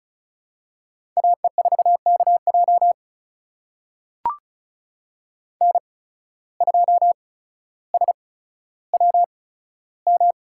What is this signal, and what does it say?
Morse code, heard as a keyed tone, about 36 wpm, spaced as for 15 wpm, 700 Hz. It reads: AE4KJ N2SWM
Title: USA Call Signs - 5 Characters - 8x Character Spacing - 35wpm